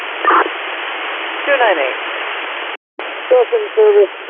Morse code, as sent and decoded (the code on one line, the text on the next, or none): none